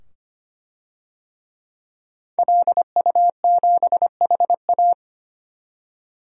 Morse code, heard as a keyed tone, about 25 wpm, 700 Hz. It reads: LU7HA
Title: DXCC-100 Call Signs - 5 Characters - 25wpm